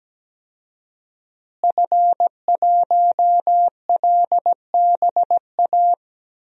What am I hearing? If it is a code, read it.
Morse code, heard as a keyed tone, about 17 wpm, 700 Hz. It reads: F1LBA
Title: DXCC-100 Call Signs - 5 Characters - Repeated 2x - 17wpm